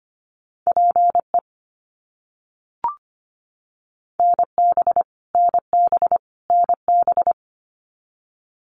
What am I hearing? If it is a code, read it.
Morse code, heard as a keyed tone, about 25 wpm, 700 Hz. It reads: PE NB NB NB
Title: Canadian Provinces and Territories - Repeated 3x - 25wpm